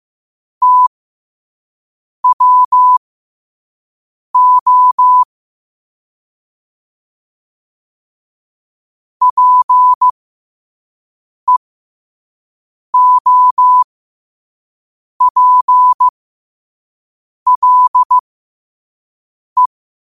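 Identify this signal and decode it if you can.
Morse code, heard as a keyed tone, about 15 wpm, spaced as for 5 wpm, 1000 Hz. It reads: TWO PEOPLE
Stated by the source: Morse code quote of the day sent at 5 wpm.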